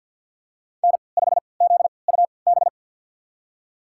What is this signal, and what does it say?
Morse code, heard as a keyed tone, about 46 wpm, spaced as for 27 wpm, 700 Hz. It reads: N5ZUB
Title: USA Call Signs - 5 Characters - 3x Character Spacing - 50wpm